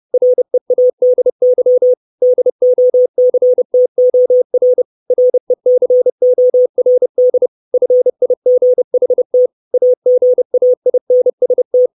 Morse code, heard as a keyed tone, about 30 wpm, 500 Hz. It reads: READY DOCTOR RECORD FIGHT AGAINST
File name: 632 Words 30WPM